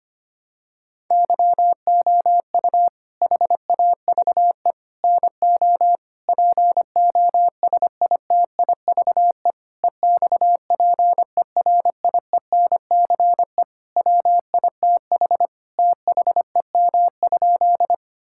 Morse code, heard as a keyed tone, about 25 wpm, 700 Hz. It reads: YOU HAVE NO POSITIVE EXPERIENCE WITH THEM?
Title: Sentences from Top 1000 Words - Review Part 5 - 25wpm